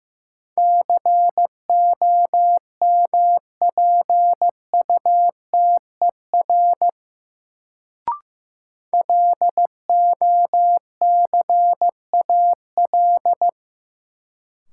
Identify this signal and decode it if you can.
Morse code, heard as a keyed tone, about 15 wpm, 700 Hz. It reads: COMPUTER LOCAL
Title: Sentences from Top 1000 Words - Review Part 3 - New Word Drill - 15wpm